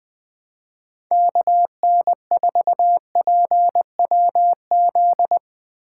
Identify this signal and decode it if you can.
Morse code, heard as a keyed tone, about 22 wpm, 700 Hz. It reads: KN4PWZ
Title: Speed Racing - USA Call Signs 20wpm